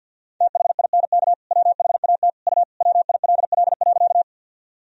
Morse code, heard as a keyed tone, about 53 wpm, 700 Hz. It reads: THINK WHAT U WILL.
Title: Sentences from Top 200 Words and Abbreviations 50wpm